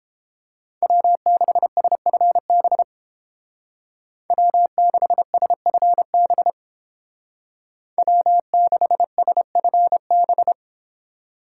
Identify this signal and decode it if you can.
Morse code, heard as a keyed tone, about 29 wpm, 700 Hz. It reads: W6SFB W6SFB W6SFB